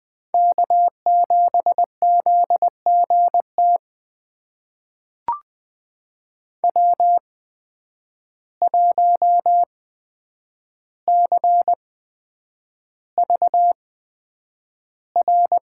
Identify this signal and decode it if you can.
Morse code, heard as a keyed tone, about 20 wpm, spaced as for 9 wpm, 700 Hz. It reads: K7ZGT W1CVR
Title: USA Call Signs - 5 Characters - 8x Character Spacing - 20wpm